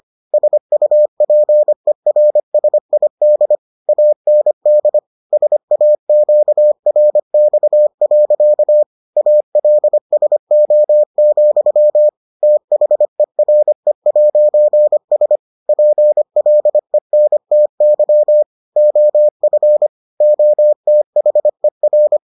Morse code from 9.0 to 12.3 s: ALSO,

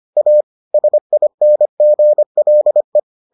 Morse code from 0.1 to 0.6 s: A

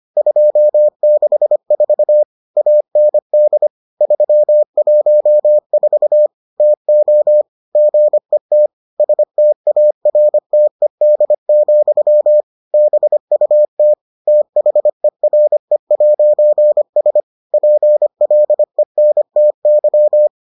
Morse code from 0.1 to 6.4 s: 264 AND 314

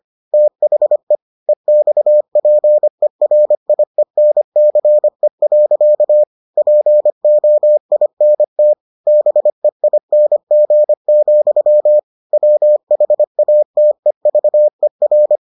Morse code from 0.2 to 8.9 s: THE EXPERIENCE. POINT

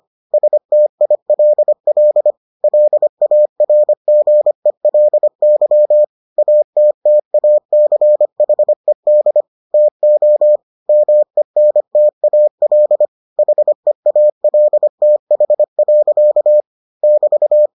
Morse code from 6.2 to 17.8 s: ATTACHED TO MENTAL HEALTH. <BT>

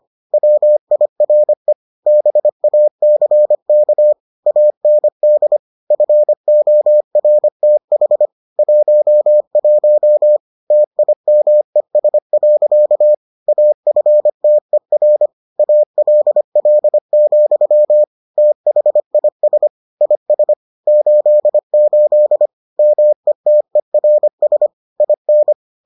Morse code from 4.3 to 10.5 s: AND FORTH 11

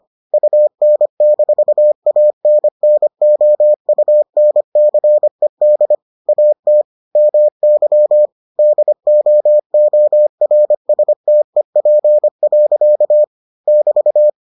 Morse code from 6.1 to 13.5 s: AT MY DOORSTEP.

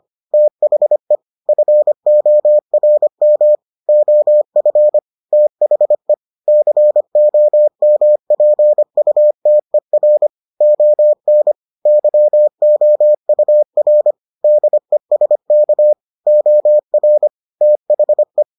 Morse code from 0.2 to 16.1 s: THE FORM OF THE COMPUTER ON YOUR DESK

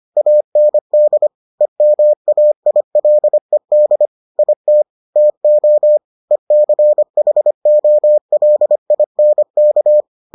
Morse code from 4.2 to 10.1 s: IT TO ECHOLINK

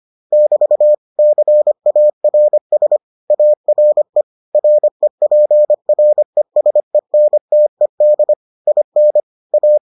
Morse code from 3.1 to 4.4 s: ARE

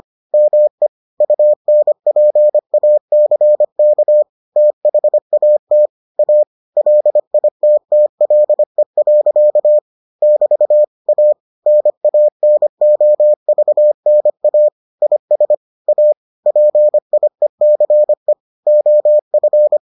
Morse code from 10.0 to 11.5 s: <BT> A

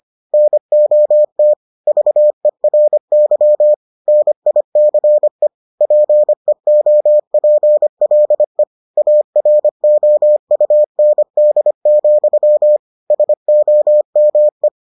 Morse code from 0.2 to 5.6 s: NOT VERY NICE

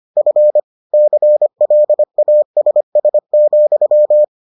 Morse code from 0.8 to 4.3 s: CLASS,

